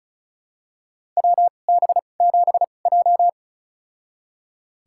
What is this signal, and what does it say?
Morse code, heard as a keyed tone, about 36 wpm, spaced as for 25 wpm, 700 Hz. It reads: WB7J
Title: USA Call Signs - 4 Characters - 2x Character Spacing - 35wpm